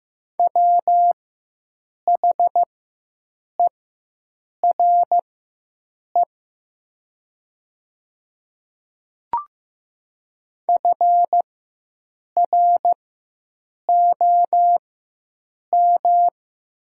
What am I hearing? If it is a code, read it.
Morse code, heard as a keyed tone, about 15 wpm, spaced as for 7 wpm, 700 Hz. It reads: WHERE FROM
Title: 4x Character Spacing - Top 300 Words - 15wpm